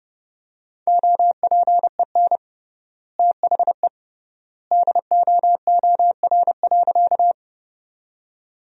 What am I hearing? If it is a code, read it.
Morse code, heard as a keyed tone, about 30 wpm, 700 Hz. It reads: OPEN THE DOOR.
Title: 3x Word-Spacing - Sentences from Top 300 Words - 30wpm